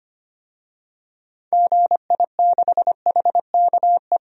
Morse code, heard as a keyed tone, about 25 wpm, 700 Hz. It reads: GI6HKE
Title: DXCC-100 Call Signs - 6 Characters - 25wpm